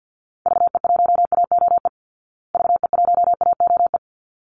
Morse code, heard as a keyed tone, about 50 wpm, 700 Hz. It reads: VE1AOE VE1AOE
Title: Call Signs Repeated 3x - 50wpm